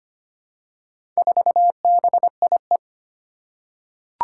Morse code, heard as a keyed tone, about 25 wpm, 700 Hz. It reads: 4BIE
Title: Four Challenging Letter-Number Combinations 25wpm